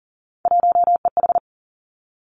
Morse code, heard as a keyed tone, about 40 wpm, 700 Hz. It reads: 1EH
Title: Three Challenging Letter-Number Combinations 40wpm